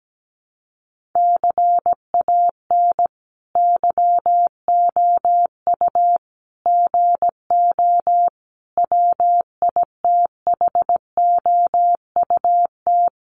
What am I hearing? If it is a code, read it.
Morse code, heard as a keyed tone, about 17 wpm, 700 Hz. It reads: CAN YOU GO WITHOUT
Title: Sentences from Top 1500 Words 17wpm